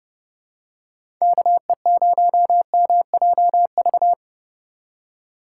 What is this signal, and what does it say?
Morse code, heard as a keyed tone, about 30 wpm, 700 Hz. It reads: KE0MJV